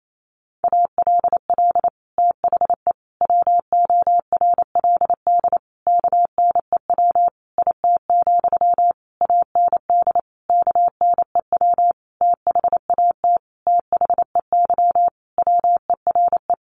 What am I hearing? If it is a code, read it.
Morse code, heard as a keyed tone, about 28 wpm, 700 Hz. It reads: ALL THE WORLD KNEW IT, AND KNEW THAT THEY WERE
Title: Sentences from Top 1200 Words 28wpm